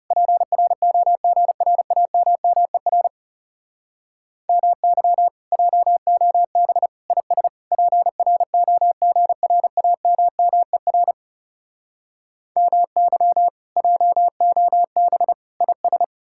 Morse code, heard as a keyed tone, about 40 wpm, 700 Hz. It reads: PROGRAMMER MY JOB IS PROGRAMMER MY JOB IS